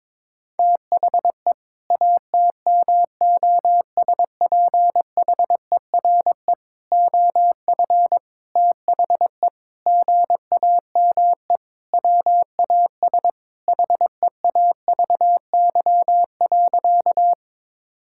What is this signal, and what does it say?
Morse code, heard as a keyed tone, about 22 wpm, 700 Hz. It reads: THE ATMOSPHERE OF THE GAME WAS HEAVY.
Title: Sentences from Top 1600 Words 22wpm